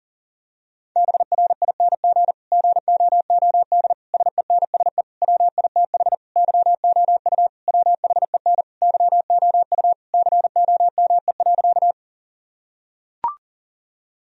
Morse code, heard as a keyed tone, about 40 wpm, 700 Hz. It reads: BRING GOOD SENSE WITH YOU WHEN YOU COME.
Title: Sentences from Top 1000 Words - Review Part 4 - 40wpm